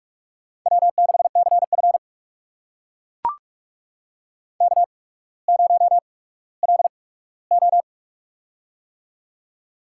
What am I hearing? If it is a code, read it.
Morse code, heard as a keyed tone, about 44 wpm, spaced as for 19 wpm, 700 Hz. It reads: W6CF K0LO
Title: USA Call Signs - 4 Characters - 8x Character Spacing - 45wpm